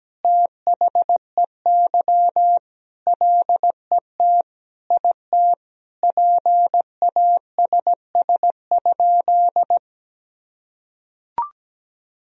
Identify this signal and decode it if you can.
Morse code, heard as a keyed tone, about 17 wpm, 700 Hz. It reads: THEY LET IT PASS?
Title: Sentences from Top 400 Words and Abbreviations 17wpm